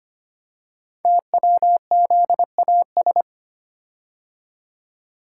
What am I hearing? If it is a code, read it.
Morse code, heard as a keyed tone, about 25 wpm, 700 Hz. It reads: TWZAS
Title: Code Groups - 5 Letters - 25wpm